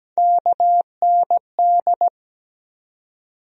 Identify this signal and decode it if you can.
Morse code, heard as a keyed tone, about 17 wpm, 700 Hz. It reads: KND
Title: USA Call Sign Postfixes - 3 Letters 17wpm